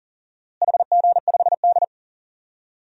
Morse code, heard as a keyed tone, about 40 wpm, 700 Hz. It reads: HG5D
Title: DXCC-100 Call Signs - 4 Characters - Repeated 3x - 40wpm